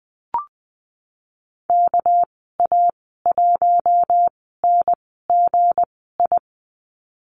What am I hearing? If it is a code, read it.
Morse code, heard as a keyed tone, about 20 wpm, spaced as for 14 wpm, 700 Hz. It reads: KA1NGI
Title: 207 Lesson 33g - Letter j - Call Signs 20wpm